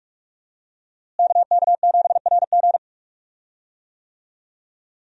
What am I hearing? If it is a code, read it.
Morse code, heard as a keyed tone, about 46 wpm, 700 Hz. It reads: KK7RG
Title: USA Call Signs - 5 Characters - Repeated 3x - 45wpm